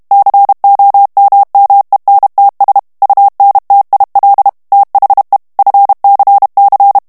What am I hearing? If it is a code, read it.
Morse code, heard as a keyed tone, about 32 wpm, 800 Hz. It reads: COMMENTS UNTIL THE FCC
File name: morse mp3 / ANGLAIS texte clair 3 vitesse 30 mots minute